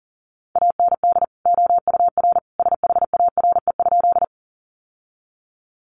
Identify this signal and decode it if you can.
Morse code, heard as a keyed tone, about 40 wpm, 700 Hz. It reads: AND OUR SHARE?
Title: Sentences from Top 900 Words 40wpm